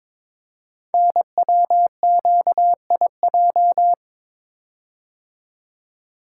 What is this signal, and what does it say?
Morse code, heard as a keyed tone, about 22 wpm, 690 Hz. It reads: NWQIJ
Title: Code Groups - 5 Letter:Numbers - 22wpm